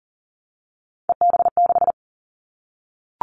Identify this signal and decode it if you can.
Morse code, heard as a keyed tone, about 37 wpm, 700 Hz. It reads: EB6
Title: Callsign Prefixes for Top 50 DXCC - 40wpm